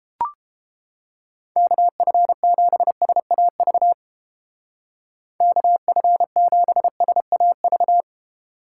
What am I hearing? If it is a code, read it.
Morse code, heard as a keyed tone, about 31 wpm, 700 Hz. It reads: KF7SAV KF7SAV